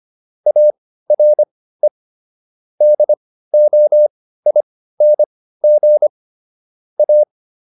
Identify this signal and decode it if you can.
Morse code, heard as a keyed tone, about 25 wpm, spaced as for 15 wpm, 600 Hz. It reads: ARE DOING A